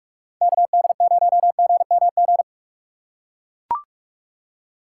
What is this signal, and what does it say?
Morse code, heard as a keyed tone, about 44 wpm, 700 Hz. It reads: KD0GMG
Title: USA Call Signs - 6 Characters - 3x Character Spacing - 45wpm